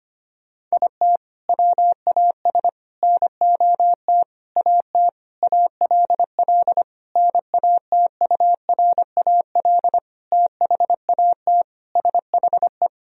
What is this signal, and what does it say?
Morse code, heard as a keyed tone, about 25 wpm, 700 Hz. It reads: IT WAS NOT AT ALL NATURAL THAT SHE
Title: Sentences from Top 1000 Words - Review Part 4 - 25wpm